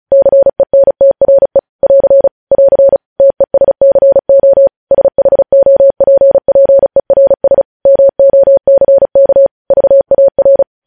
Morse code, heard as a keyed tone, about 36 wpm, 550 Hz. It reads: CENTRE <AR> <AR> TESCO SHOPPERS MOCK VAR
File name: podcast / News headlines-1x-35x35